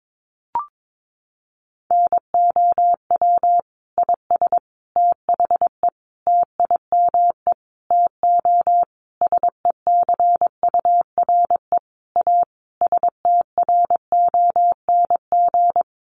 Morse code, heard as a keyed tone, about 22 wpm, 700 Hz. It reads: NOW IS THE TIME TO SECURE A STRONG
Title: Sentences from Top 1200 Words 22wpm